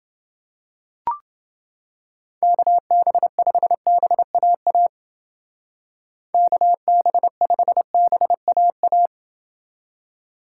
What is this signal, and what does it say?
Morse code, heard as a keyed tone, about 29 wpm, 700 Hz. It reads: KB5BAA KB5BAA